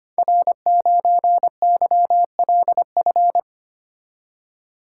Morse code, heard as a keyed tone, about 25 wpm, 700 Hz. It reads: R9YLF
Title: Code Groups - 5 Letter:Numbers - 25wpm